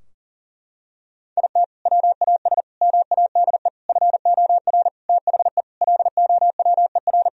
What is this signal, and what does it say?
Morse code, heard as a keyed tone, about 40 wpm, 700 Hz. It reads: IT WAS MADE FOR THE LOWER
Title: Sentences from Top 1000 Words - Review Part 4 - 40wpm